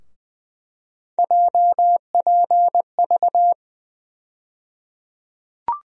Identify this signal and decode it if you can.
Morse code, heard as a keyed tone, about 20 wpm, 700 Hz. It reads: JPV
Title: USA Call Sign Postfixes - 3 Letters 20wpm